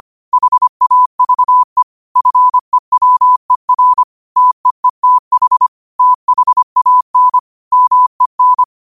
Morse code, heard as a keyed tone, about 25 wpm, 1000 Hz. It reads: HAVE FEWER TEETH THAN MEN
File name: qotdfiles / 25wpm